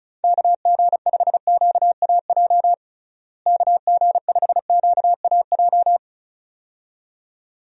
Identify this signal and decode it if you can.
Morse code, heard as a keyed tone, about 34 wpm, 700 Hz. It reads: KG5QAJ KG5QAJ
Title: USA Call Signs - 6 Characters - Repeated 2x - 35wpm